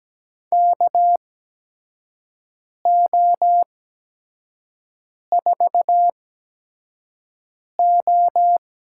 Morse code, heard as a keyed tone, about 17 wpm, spaced as for 5 wpm, 700 Hz. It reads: KO4O
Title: USA Call Signs - 5 Characters - 8x Character Spacing - 17wpm